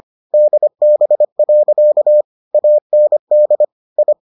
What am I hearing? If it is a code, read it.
Morse code, heard as a keyed tone, about 25 wpm, 600 Hz. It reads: DB. AND I